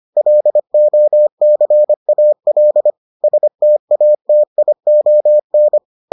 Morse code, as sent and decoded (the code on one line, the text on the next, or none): .-.. --- -.-. .- .-.. / ... - .- - .. --- -.
LOCAL STATION